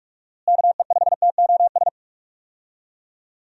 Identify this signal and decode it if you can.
Morse code, heard as a keyed tone, about 45 wpm, 700 Hz. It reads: KE5TOS
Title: USA Call Signs - 6 Characters - Repeated 2x - 45wpm